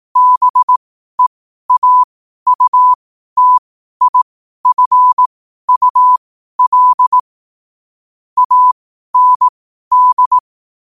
Morse code, heard as a keyed tone, about 18 wpm, spaced as for 12 wpm, 1000 Hz. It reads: BEAUTIFUL AND